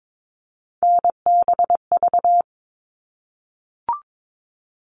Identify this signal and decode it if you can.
Morse code, heard as a keyed tone, about 22 wpm, 700 Hz. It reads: NBV
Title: All Cut Numbers - 3-Digit 22wpm